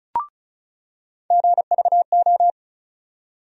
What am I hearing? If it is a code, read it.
Morse code, heard as a keyed tone, about 36 wpm, 700 Hz. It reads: GVO
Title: USA Call Sign Postfixes - 3 Letters 35wpm